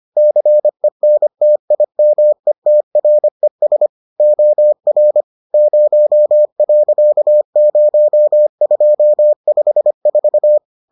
Morse code, heard as a keyed tone, about 25 wpm, 600 Hz. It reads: CENTIMETRES OR 0.0254